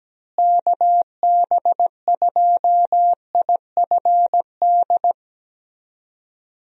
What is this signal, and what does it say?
Morse code, heard as a keyed tone, about 17 wpm, 700 Hz. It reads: KB2IFD
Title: USA Call Signs - 6 Characters - Repeated 2x - 17wpm